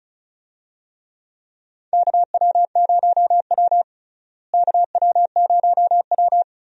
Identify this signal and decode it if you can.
Morse code, heard as a keyed tone, about 36 wpm, 700 Hz. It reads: KW0W KW0W
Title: USA Call Signs - 4 Characters - Repeated 2x - 35wpm